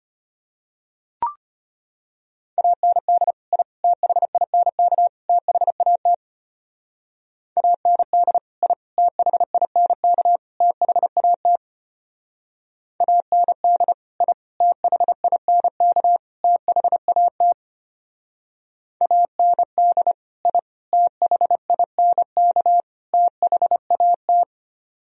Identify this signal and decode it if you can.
Morse code, heard as a keyed tone, about 31 wpm, 700 Hz. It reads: AND I THINK THAT AND I THINK THAT AND I THINK THAT AND I THINK THAT